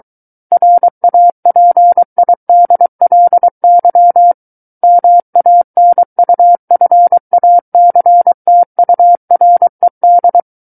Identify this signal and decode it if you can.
Morse code, heard as a keyed tone, about 23 wpm, 700 Hz. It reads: RAPIDLY MANUFACTURED
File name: Book One / 14-Chapter 14 - 23 WPM